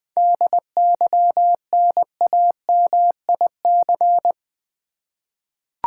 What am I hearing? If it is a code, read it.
Morse code, heard as a keyed tone, about 20 wpm, 700 Hz. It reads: DYNAMIC
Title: 7 Letter Words - First 3 Letters Given - 20wpm